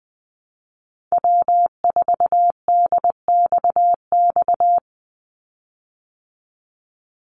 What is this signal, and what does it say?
Morse code, heard as a keyed tone, about 20 wpm, 700 Hz. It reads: W4DXX
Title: Call Signs Repeated 2x - 20wpm